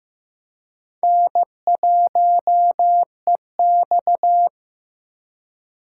Morse code, heard as a keyed tone, about 18 wpm, 700 Hz. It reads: N1EX
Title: Speed Racing - USA 1x2 Call Signs 15wpm